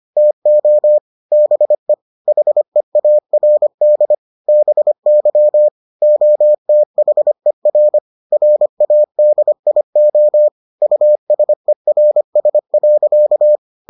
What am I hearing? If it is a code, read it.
Morse code, heard as a keyed tone, about 25 wpm, 600 Hz. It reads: TO BE HEARD BY OTHER RADIO USERS.